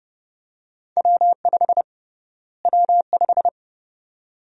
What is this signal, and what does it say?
Morse code, heard as a keyed tone, about 30 wpm, 700 Hz. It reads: W5 W5
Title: USA Call Sign Prefixes - Repeated 2x - 30wpm